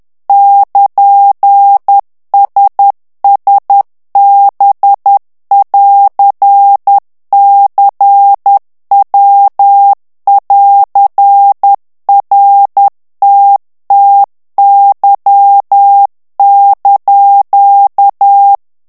Morse code, heard as a keyed tone, about 10 wpm, 800 Hz. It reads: <KN>SSB<AR>CW<AR>RTTY)
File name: ITALIEN texte clair 2 vitesse 10 mots minute